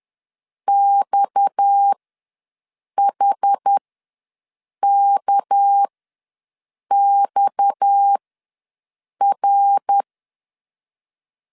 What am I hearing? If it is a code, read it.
Morse code, heard as a keyed tone, about 10 wpm, spaced as for 6 wpm, 800 Hz. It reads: XHKXR
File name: LEZIONI / 6 PKX0 / 6 PKX ws 10wpm 3706